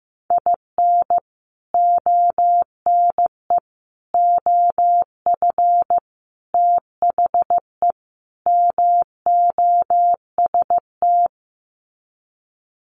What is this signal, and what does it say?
Morse code, heard as a keyed tone, about 15 wpm, 700 Hz. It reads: IN ONE OF THE MOST
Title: Sets of 5 Words Encore 15wpm